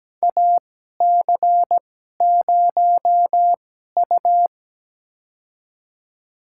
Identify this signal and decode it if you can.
Morse code, heard as a keyed tone, about 17 wpm, spaced as for 12 wpm, 700 Hz. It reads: AC0U
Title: USA Call Signs - 4 Characters - 2x Character Spacing - 17wpm